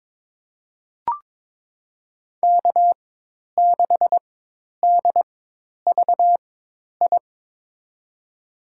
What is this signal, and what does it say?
Morse code, heard as a keyed tone, about 21 wpm, spaced as for 10 wpm, 700 Hz. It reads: K6DVI
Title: USA Call Signs - 5 Characters - 4x Character Spacing - 22wpm